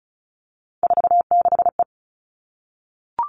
Morse code, heard as a keyed tone, about 34 wpm, 700 Hz. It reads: 46E
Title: All Cut Numbers - 3-Digit 35wpm